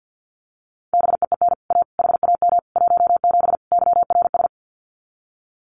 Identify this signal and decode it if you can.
Morse code, heard as a keyed tone, about 50 wpm, 700 Hz. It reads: BEEN A HAM 17 YRS
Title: Common QSO Phrases Encore 50wpm